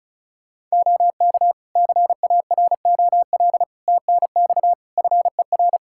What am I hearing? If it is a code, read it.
Morse code, heard as a keyed tone, about 36 wpm, 700 Hz. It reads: OK CAROL TNX FER
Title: K1USN SST QSO Practice 35wpm